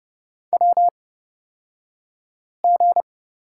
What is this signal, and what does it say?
Morse code, heard as a keyed tone, about 30 wpm, spaced as for 5 wpm, 700 Hz. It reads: WG
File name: Lesson 38d - Letter q - Two Characters 30wpm